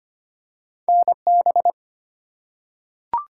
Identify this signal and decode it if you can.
Morse code, heard as a keyed tone, about 25 wpm, 700 Hz. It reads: NB
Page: morse-code-ninja-practice-25wpm